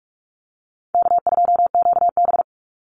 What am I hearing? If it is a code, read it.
Morse code, heard as a keyed tone, about 44 wpm, 700 Hz. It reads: K2QB
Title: Easy Call Signs Repeated 2x - 45wpm